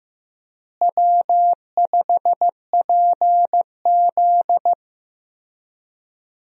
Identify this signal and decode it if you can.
Morse code, heard as a keyed tone, about 15 wpm, 700 Hz. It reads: W5PZ
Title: USA Call Signs - 4 Characters - Repeated 3x - 15wpm